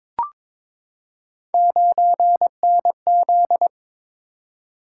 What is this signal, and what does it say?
Morse code, heard as a keyed tone, about 22 wpm, 700 Hz. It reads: 9NZ
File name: practice / 22wpm / Three Letter-Number Combinations - 22wpm